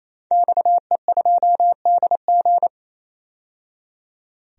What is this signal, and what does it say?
Morse code, heard as a keyed tone, about 27 wpm, 700 Hz. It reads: XE2DG
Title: DXCC-100 Call Signs - 5 Characters - 28wpm